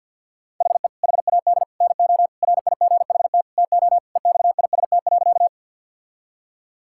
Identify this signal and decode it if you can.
Morse code, heard as a keyed tone, about 51 wpm, 690 Hz. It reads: HE HAD NO RIGHT TO EXIST.
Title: Sentences from Top 1000 Words - Review Part 5 - 50wpm